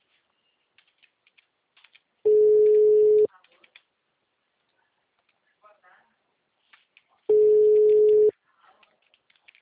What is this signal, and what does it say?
Ringback tone (Europe, 425 Hz), with speech.